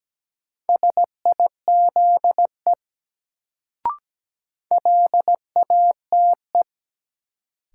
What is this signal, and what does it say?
Morse code, heard as a keyed tone, about 17 wpm, 700 Hz. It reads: SIZE LATE
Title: Sets of 3 Words - New Word Drill - 17wpm